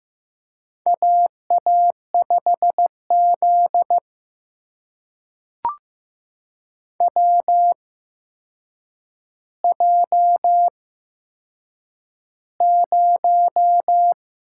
Morse code, heard as a keyed tone, about 15 wpm, spaced as for 7 wpm, 700 Hz. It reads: AA5Z WJ0
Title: USA Call Signs - 4 Characters - 8x Character Spacing - 15wpm